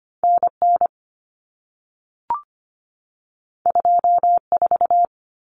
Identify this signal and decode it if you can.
Morse code, heard as a keyed tone, about 25 wpm, 700 Hz. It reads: NN 24
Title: Common Cut Numbers - 2-Digit 25wpm